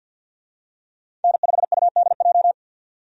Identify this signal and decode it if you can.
Morse code, heard as a keyed tone, about 52 wpm, 700 Hz. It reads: N5UDJ
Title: USA Call Signs - 5 Characters - 50wpm